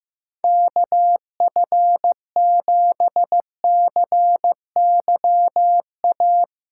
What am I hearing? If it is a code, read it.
Morse code, heard as a keyed tone, about 15 wpm, 700 Hz. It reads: KF7CYA
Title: USA Callsign DE Callsign - 6 Characters - 15wpm